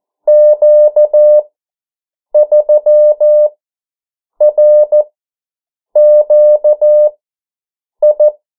Morse code, heard as a keyed tone, about 14 wpm, spaced as for 7 wpm, 600 Hz. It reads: Q3RQI
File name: morse mp3 / 09 urok 8